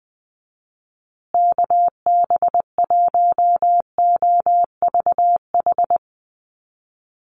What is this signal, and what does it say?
Morse code, heard as a keyed tone, about 20 wpm, 700 Hz. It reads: KB1OVH
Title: USA Call Signs Repeated 2x 20wpm